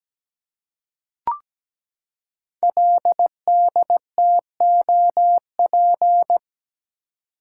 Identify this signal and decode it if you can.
Morse code, heard as a keyed tone, about 17 wpm, 700 Hz. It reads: LDTOP
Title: Code Groups - 5 Letters - 17wpm